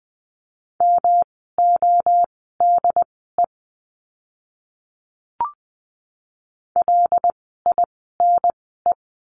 Morse code, heard as a keyed tone, about 20 wpm, spaced as for 15 wpm, 700 Hz. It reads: MODE LINE
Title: 108 Lesson 18f - Letter m - Words 20wpm